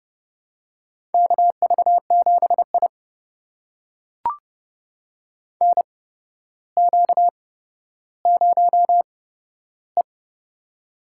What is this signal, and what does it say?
Morse code, heard as a keyed tone, about 30 wpm, spaced as for 13 wpm, 700 Hz. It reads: KV7I NQ0E
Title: USA Call Signs - 4 Characters - 8x Character Spacing - 30wpm